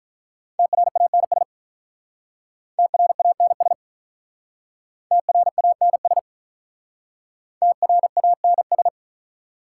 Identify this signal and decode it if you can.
Morse code, heard as a keyed tone, about 46 wpm, 700 Hz. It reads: TRANS TRANS TRANS TRANS